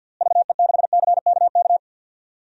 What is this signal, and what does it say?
Morse code, heard as a keyed tone, about 49 wpm, 700 Hz. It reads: VE6CKK
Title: DXCC-100 Call Signs - 6 Characters - Repeated 3x - 50wpm